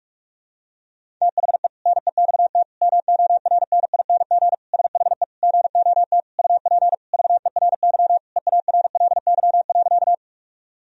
Morse code, heard as a keyed tone, about 46 wpm, 700 Hz. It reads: THE NEXT MORNING SHE GOT UP VERY EARLY.
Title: Sentences from Top 1000 Words - Review Part 2 - 45wpm